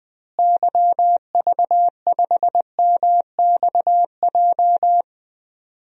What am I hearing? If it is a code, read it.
Morse code, heard as a keyed tone, about 21 wpm, 700 Hz. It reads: YV5MXJ
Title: DXCC-100 Call Signs - 6 Characters - Repeated 2x - 20wpm